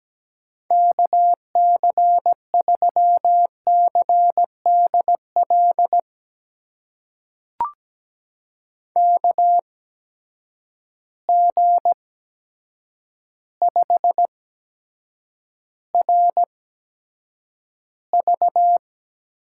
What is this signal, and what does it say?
Morse code, heard as a keyed tone, about 17 wpm, spaced as for 8 wpm, 700 Hz. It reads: KC3CDL KG5RV
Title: USA Call Signs - 6 Characters - 8x Character Spacing - 17wpm